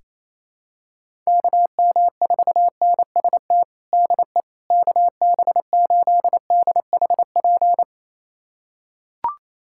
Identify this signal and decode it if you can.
Morse code, heard as a keyed tone, about 28 wpm, 700 Hz. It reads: KM4NST DE KB8DHP
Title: USA Callsign DE Callsign 28wpm